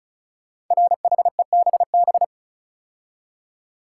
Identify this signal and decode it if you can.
Morse code, heard as a keyed tone, about 36 wpm, 700 Hz. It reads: RHEBB
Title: Code Groups - 5 Letters - 35wpm